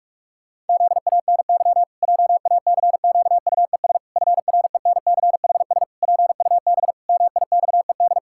Morse code, heard as a keyed tone, about 41 wpm, 700 Hz. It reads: ZANY JACQUES FRENCHS PUB MIXED
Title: Pangram Challenge v2 - 45wpm